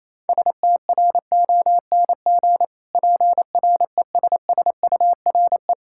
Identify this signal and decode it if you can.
Morse code, heard as a keyed tone, about 28 wpm, 700 Hz. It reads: STRONG PRESSURE
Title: Sentences from Top 1000 Words - Review Part 6 - 28wpm